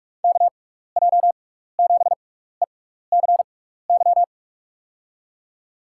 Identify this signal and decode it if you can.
Morse code, heard as a keyed tone, about 45 wpm, spaced as for 15 wpm, 700 Hz. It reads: KJ7ECY
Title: USA Call Signs - 6 Characters - 6x Character Spacing - 45wpm